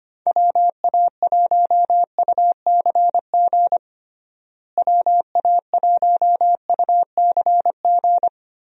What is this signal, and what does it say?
Morse code, heard as a keyed tone, about 25 wpm, 700 Hz. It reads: WA1UCG WA1UCG
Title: USA Call Signs - 6 Characters - Repeated 2x - 25wpm